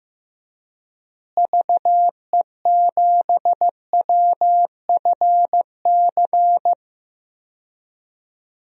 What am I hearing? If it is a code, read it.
Morse code, heard as a keyed tone, about 15 wpm, 700 Hz. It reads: VE7WFC
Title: DXCC-100 Call Signs - 6 Characters - Repeated 3x - 15wpm